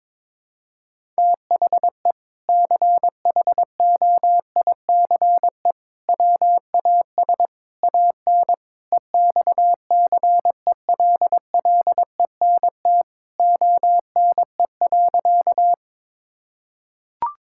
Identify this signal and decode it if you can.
Morse code, heard as a keyed tone, about 21 wpm, 700 Hz. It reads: THE CHOICE WAS AN EXCELLENT ONE.
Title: Sentences from Top 1000 Words - Review Part 6 - 22wpm